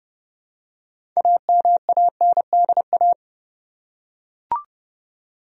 Morse code, heard as a keyed tone, about 30 wpm, 700 Hz. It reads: AMANDA
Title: USA Common Names - 30wpm